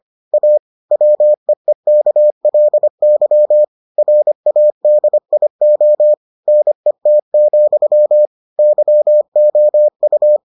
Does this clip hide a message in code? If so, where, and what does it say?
Yes: Morse code from 0.3 to 10.4 s, heard as a keyed tone, about 25 wpm, 600 Hz. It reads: A WEEKLY RADIO NET, YOU